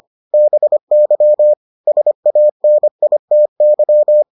Morse code, heard as a keyed tone, about 25 wpm, 600 Hz. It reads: BY SANITY